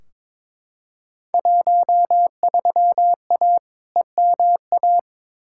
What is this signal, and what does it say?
Morse code, heard as a keyed tone, about 21 wpm, 700 Hz. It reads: 13A EMA
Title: Field Day Exchanges Encore - 22wpm